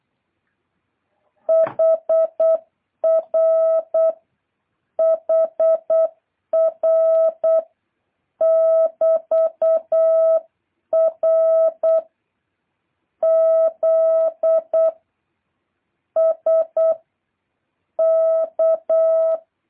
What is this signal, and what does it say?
Morse code, heard as a keyed tone, about 8 wpm, 650 Hz. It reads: HR HR <BT>R Z S K
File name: jh8cbh / 2023cw